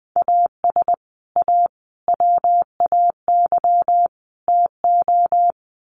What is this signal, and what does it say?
Morse code, heard as a keyed tone, about 20 wpm, 700 Hz. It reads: AS A WAY TO
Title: Sets of 4 Words 20wpm